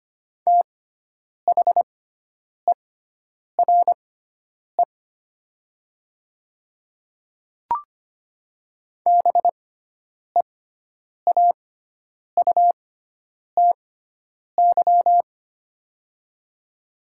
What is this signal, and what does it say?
Morse code, heard as a keyed tone, about 25 wpm, spaced as for 9 wpm, 700 Hz. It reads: THERE BEAUTY
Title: 6x Character Spacing - Top 500 Words - 25wpm